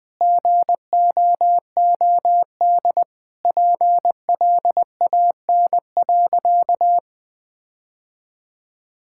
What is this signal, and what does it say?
Morse code, heard as a keyed tone, about 21 wpm, 700 Hz. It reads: GOOD PLAN.
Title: Sentences with 5-Word Phrases - 20wpm